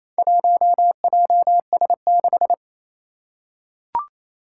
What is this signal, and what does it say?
Morse code, heard as a keyed tone, about 27 wpm, 700 Hz. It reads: 1JS6
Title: Four Challenging Letter-Number Combinations 28wpm